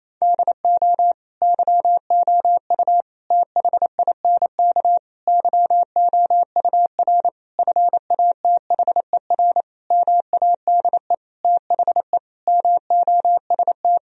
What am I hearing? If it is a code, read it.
Morse code, heard as a keyed tone, about 28 wpm, 700 Hz. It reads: DO YOU THINK YOUR FATHER MADE THE MOST
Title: Sentences from Top 1000 Words - Review Part 1 - 28wpm